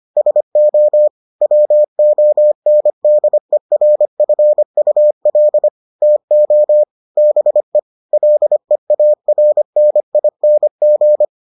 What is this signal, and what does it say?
Morse code, heard as a keyed tone, about 25 wpm, 600 Hz. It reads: SO WONDERFUL TO BE LEARNING